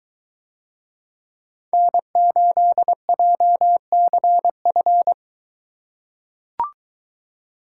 Morse code, heard as a keyed tone, about 23 wpm, 700 Hz. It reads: N8JCF